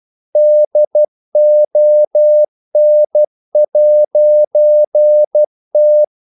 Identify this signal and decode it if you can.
Morse code, heard as a keyed tone, about 12 wpm, 600 Hz. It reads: DON'T
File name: CW / Practice / EnglishQuotations0312